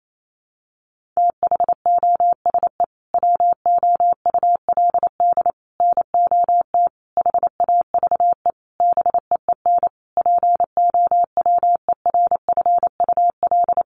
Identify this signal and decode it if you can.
Morse code, heard as a keyed tone, about 28 wpm, 700 Hz. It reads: THOSE WOULD NOT HAVE BEEN POWERFUL
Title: Sentences from Top 1300 Words 28wpm